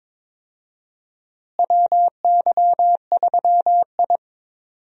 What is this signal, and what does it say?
Morse code, heard as a keyed tone, about 22 wpm, 700 Hz. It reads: WY3I
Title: USA Call Signs - 4 Characters - Repeated 3x - 22wpm